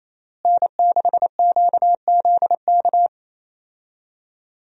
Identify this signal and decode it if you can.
Morse code, heard as a keyed tone, about 27 wpm, 700 Hz. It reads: N6QZK
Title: USA Call Signs - 5 Characters - Repeated 3x - 28wpm